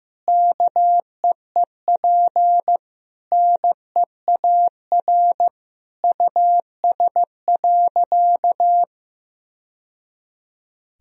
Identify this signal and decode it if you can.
Morse code, heard as a keyed tone, about 15 wpm, 700 Hz. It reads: KEEP NEAR US.
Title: Sentences from Top 1000 Words - Review Part 2 - 15wpm